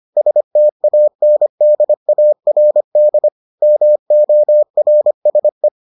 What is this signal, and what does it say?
Morse code, heard as a keyed tone, about 25 wpm, 600 Hz. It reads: STANDARD MORSE